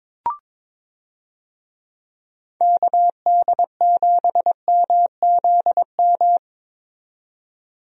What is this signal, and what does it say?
Morse code, heard as a keyed tone, about 22 wpm, 700 Hz. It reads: KD7MZM
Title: USA Call Signs - 6 Characters - 22wpm